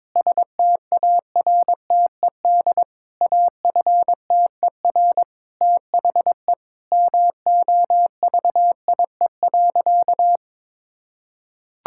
Morse code, heard as a keyed tone, about 21 wpm, 700 Hz. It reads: STARTED AFTER THE MOVIE.
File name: practice / 22wpm / Sentences with 2-Word Phrases - 22wpm